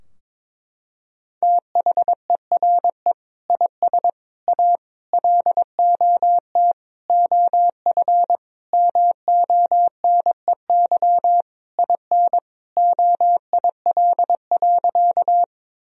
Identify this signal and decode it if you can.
Morse code, heard as a keyed tone, about 22 wpm, 700 Hz. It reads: THERE IS A LOT OF MONEY IN OIL.
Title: Sentences from Top 1000 Words - Review Part 4 - 22wpm